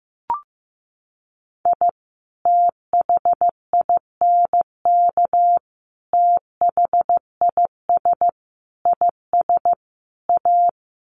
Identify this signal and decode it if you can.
Morse code, heard as a keyed tone, about 15 wpm, 700 Hz. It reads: I THINK THIS IS A
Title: Sets of 5 Words 15wpm